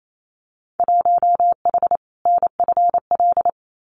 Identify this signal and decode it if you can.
Morse code, heard as a keyed tone, about 28 wpm, 700 Hz. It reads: 1H NFL
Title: Winter Field Day Exchanges 28wpm